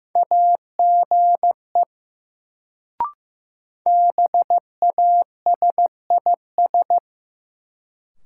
Morse code, heard as a keyed tone, about 15 wpm, 700 Hz. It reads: AGE BASIS
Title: Sets of 3 Words - New Word Drill - 15wpm